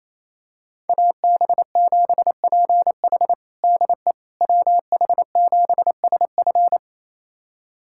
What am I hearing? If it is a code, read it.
Morse code, heard as a keyed tone, about 28 wpm, 700 Hz. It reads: AB7PH DE WH7SF
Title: USA Callsign DE Callsign - 5 Characters - 28wpm